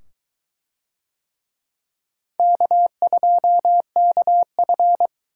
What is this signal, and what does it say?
Morse code, heard as a keyed tone, about 23 wpm, 700 Hz. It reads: K2KF